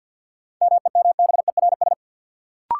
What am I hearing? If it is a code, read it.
Morse code, heard as a keyed tone, about 53 wpm, 700 Hz. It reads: MEMBERS
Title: Sets of 3 Words - New Word Drill - 50wpm